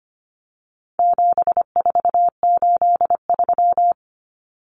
Morse code, heard as a keyed tone, about 25 wpm, 700 Hz. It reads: 7483
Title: Common Cut Numbers - 4-Digit 25wpm